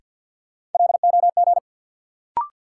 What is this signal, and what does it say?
Morse code, heard as a keyed tone, about 51 wpm, 700 Hz. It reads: LOG
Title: Sentences from Top 1400 Words - New Word Drill Alt - 50wpm